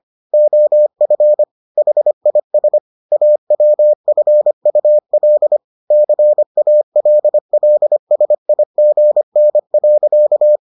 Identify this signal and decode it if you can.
Morse code, heard as a keyed tone, about 25 wpm, 600 Hz. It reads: OF HIS AWFUL CALLSIGN.